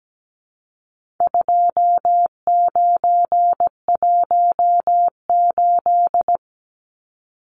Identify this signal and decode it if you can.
Morse code, heard as a keyed tone, about 17 wpm, 700 Hz. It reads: 2918
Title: Numbers - 4-Digit 17wpm